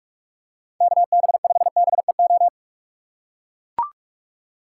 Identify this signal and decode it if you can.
Morse code, heard as a keyed tone, about 44 wpm, 700 Hz. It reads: KB5BEO